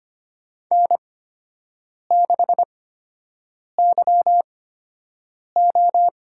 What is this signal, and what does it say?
Morse code, heard as a keyed tone, about 25 wpm, spaced as for 7 wpm, 700 Hz. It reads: N6YO
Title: USA Call Signs - 8x Character Spacing - 25wpm